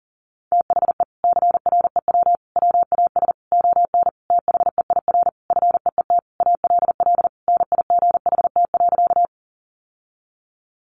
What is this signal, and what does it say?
Morse code, heard as a keyed tone, about 40 wpm, 700 Hz. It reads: THE CREW WAS ON THEIR FEET ALL NIGHT.
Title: Sentences from Top 1500 Words 40wpm